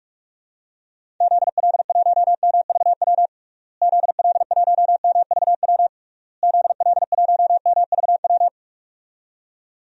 Morse code, heard as a keyed tone, about 47 wpm, 700 Hz. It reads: ZL1MVW ZL1MVW ZL1MVW
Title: DXCC-100 Call Signs - 6 Characters - Repeated 3x - 45wpm